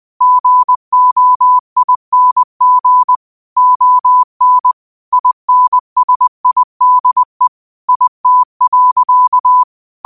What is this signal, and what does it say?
Morse code, heard as a keyed tone, about 20 wpm, 1000 Hz. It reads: GOING ON INSIDE IT.
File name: qotdfiles / 20wpm